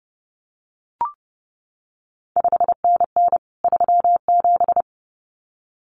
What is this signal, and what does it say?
Morse code, heard as a keyed tone, about 30 wpm, 700 Hz. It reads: 5NN 37
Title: CQ WW DX Contest Exchange Encore - 30wpm